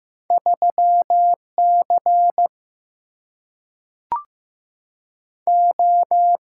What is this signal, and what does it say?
Morse code, heard as a keyed tone, about 15 wpm, 700 Hz. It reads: 3C O
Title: Single-Letter Single-Number 15wpm